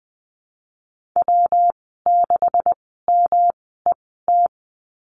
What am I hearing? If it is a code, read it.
Morse code, heard as a keyed tone, about 20 wpm, spaced as for 14 wpm, 700 Hz. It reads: W6MET
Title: 161 Lesson 26g - Letter g - Call Signs 20wpm